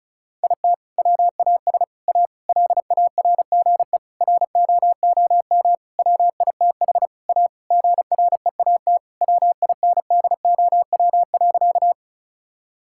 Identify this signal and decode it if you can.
Morse code, heard as a keyed tone, about 36 wpm, 700 Hz. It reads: IT WAS A LARGE ROOM WITH A GREAT WINDOW.
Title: Sentences from Top 1600 Words 35wpm